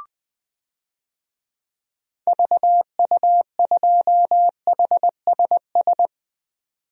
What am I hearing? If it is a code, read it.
Morse code, heard as a keyed tone, about 20 wpm, 700 Hz. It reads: VU2HSS
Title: DXCC-100 Call Signs - 6 Characters - 20wpm